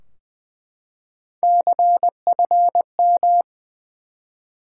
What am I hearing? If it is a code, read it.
Morse code, heard as a keyed tone, about 20 wpm, 700 Hz. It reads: CFM
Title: Top 90 QSO Elements 20wpm